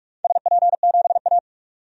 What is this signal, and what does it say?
Morse code, heard as a keyed tone, about 46 wpm, 700 Hz. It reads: SP7A
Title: DXCC-100 Call Signs - 4 Characters - 45wpm